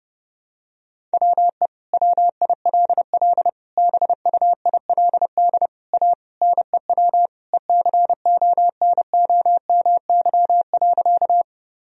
Morse code, heard as a keyed tone, about 30 wpm, 700 Hz. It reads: WE WILL BUILD A NEW ECONOMY.
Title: Sentences from Top 1000 Words - Review Part 3 - 30wpm